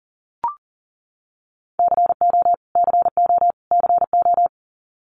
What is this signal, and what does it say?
Morse code, heard as a keyed tone, about 40 wpm, 700 Hz. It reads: CO CO CO
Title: US States Repeated 3x 40wpm